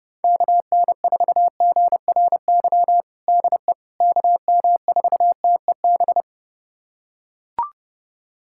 Morse code, heard as a keyed tone, about 30 wpm, 700 Hz. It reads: KN4GRY DE KM4TEB
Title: USA Callsign DE Callsign 30wpm